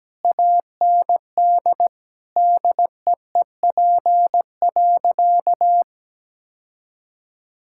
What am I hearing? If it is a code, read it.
Morse code, heard as a keyed tone, about 17 wpm, 700 Hz. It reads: AND DEEP.
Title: Sentences from Top 1000 Words - Review Part 6 - 17wpm